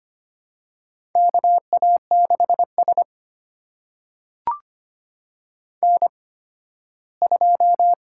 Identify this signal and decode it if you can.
Morse code, heard as a keyed tone, about 25 wpm, spaced as for 15 wpm, 700 Hz. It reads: KA6S N2
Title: USA Call Signs - 4 Characters - 8x Character Spacing - 25wpm